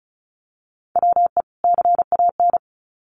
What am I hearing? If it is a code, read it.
Morse code, heard as a keyed tone, about 34 wpm, 700 Hz. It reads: WE CAN
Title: Sets of 2 Words 35wpm